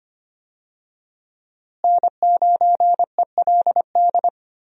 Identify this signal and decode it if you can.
Morse code, heard as a keyed tone, about 25 wpm, 700 Hz. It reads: N9ELD
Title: USA Call Signs - 5 Characters - Repeated 2x - 25wpm